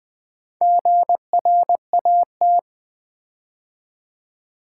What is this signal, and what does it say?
Morse code, heard as a keyed tone, about 20 wpm, 700 Hz. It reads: GRAT
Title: Build A Word - 4 Letters At A Time - 8-Letter Words - 20wpm